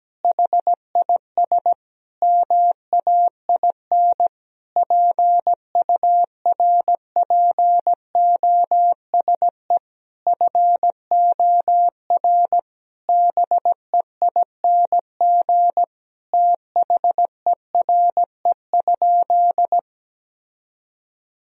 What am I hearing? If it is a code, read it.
Morse code, heard as a keyed tone, about 17 wpm, 700 Hz. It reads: HIS MAIN PURPOSE FOR BEING THERE?
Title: Sentences from Top 1000 Words - Review Part 2 - 17wpm